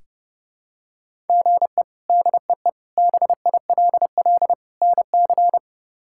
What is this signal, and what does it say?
Morse code, heard as a keyed tone, about 30 wpm, 700 Hz. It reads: GE DEE BILL NC
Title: K1USN SST QSO Practice - 30wpm